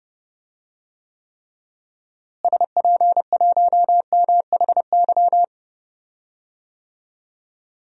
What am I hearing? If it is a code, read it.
Morse code, heard as a keyed tone, about 30 wpm, 700 Hz. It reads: SP1MHY
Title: DXCC-100 Call Signs - 6 Characters - Repeated 3x - 30wpm